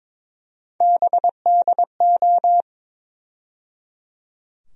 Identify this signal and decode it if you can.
Morse code, heard as a keyed tone, about 21 wpm, 700 Hz. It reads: BDO
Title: USA Call Sign Postfixes - 3 Letters 22wpm